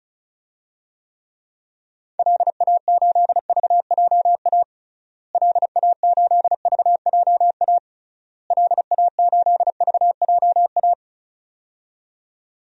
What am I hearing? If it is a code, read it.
Morse code, heard as a keyed tone, about 36 wpm, 700 Hz. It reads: LA8VJA LA8VJA LA8VJA
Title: DXCC-100 Call Signs - 6 Characters - Repeated 3x - 35wpm